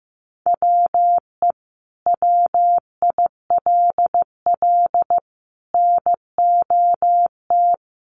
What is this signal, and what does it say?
Morse code, heard as a keyed tone, about 15 wpm, 700 Hz. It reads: WE WILL NOT
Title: Sentences from Top 1300 Words 15wpm